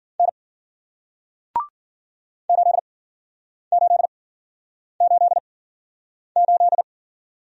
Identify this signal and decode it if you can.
Morse code, heard as a keyed tone, about 51 wpm, spaced as for 9 wpm, 700 Hz. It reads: N 8888